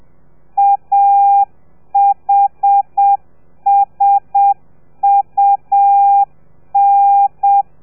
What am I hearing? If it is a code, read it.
Morse code, heard as a keyed tone, about 7 wpm, 800 Hz. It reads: AHSUN